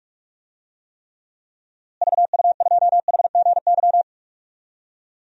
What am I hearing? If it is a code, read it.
Morse code, heard as a keyed tone, about 46 wpm, 700 Hz. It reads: VU2HGY
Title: DXCC-100 Call Signs - 6 Characters - Repeated 3x - 45wpm